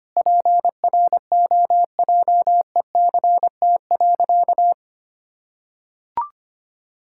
Morse code, heard as a keyed tone, about 25 wpm, 700 Hz. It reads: PROJECT.
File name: practice / 25wpm / Sentences with 9-Word Phrases - 25wpm